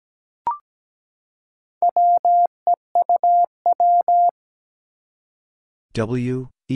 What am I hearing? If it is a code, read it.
Morse code, heard as a keyed tone, about 17 wpm, 700 Hz. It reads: WEUW
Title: Four Challenging Letter-Number Combinations 17wpm